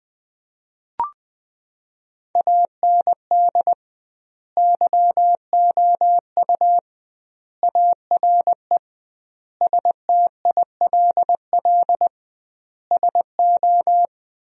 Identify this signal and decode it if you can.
Morse code, heard as a keyed tone, about 19 wpm, 700 Hz. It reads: AND YOU ARE STILL SO
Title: 2x Word-Spacing - Sentences from Top 400 Words - 20wpm